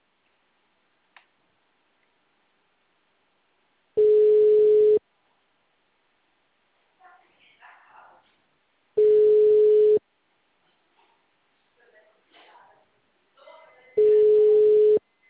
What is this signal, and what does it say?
Ringback tone (Europe, 425 Hz), with speech.
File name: out- -2006- - - .8934